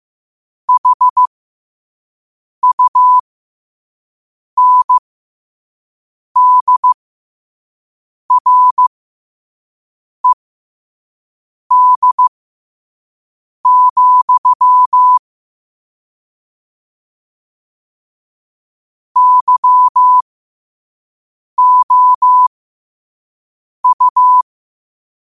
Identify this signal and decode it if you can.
Morse code, heard as a keyed tone, about 15 wpm, spaced as for 5 wpm, 1000 Hz. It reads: HUNDRED, YOU
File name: qotdfiles / 5wpm